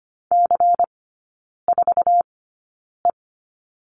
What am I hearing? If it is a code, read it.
Morse code, heard as a keyed tone, about 25 wpm, spaced as for 9 wpm, 700 Hz. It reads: C4E
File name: Lesson 17e - Letter u - Three Characters 25wpm